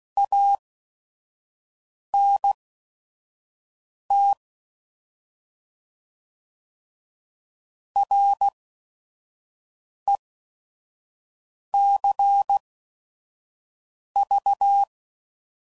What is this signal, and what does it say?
Morse code, heard as a keyed tone, about 16 wpm, spaced as for 5 wpm, 800 Hz. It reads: ANT RECV
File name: Mixed / Abreviations / Abreviations